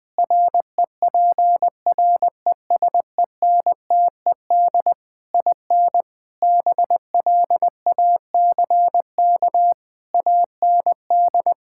Morse code, heard as a keyed tone, about 20 wpm, 700 Hz. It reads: REPRESENTED IN BLACK AND
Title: Sentences with 5-Word Phrases - 20wpm